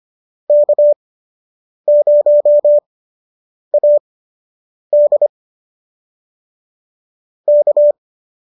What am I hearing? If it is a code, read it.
Morse code, heard as a keyed tone, about 25 wpm, spaced as for 8 wpm, 590 Hz. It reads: K0AD K